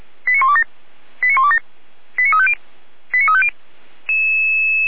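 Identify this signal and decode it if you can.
Five-tone selective call (ZVEI), heard as quick runs of five tones, tones of about 80 ms. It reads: 89127 89127 89350 89350
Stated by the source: ZVEI five-tone call received by SDR, from the audio server of feuerwehr-orlamuende.org (fire brigade Orlamünde).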